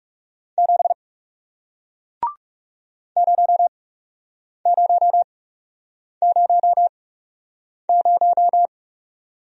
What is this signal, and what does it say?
Morse code, heard as a keyed tone, about 41 wpm, spaced as for 8 wpm, 700 Hz. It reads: 7 0000